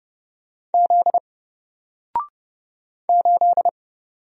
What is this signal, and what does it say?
Morse code, heard as a keyed tone, about 29 wpm, 700 Hz. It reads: Z 8
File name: Single-Letter-Number-30wpm